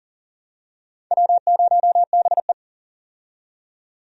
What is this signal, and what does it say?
Morse code, heard as a keyed tone, about 40 wpm, 700 Hz. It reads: W0BE
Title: USA Call Signs - 4 Characters - 40wpm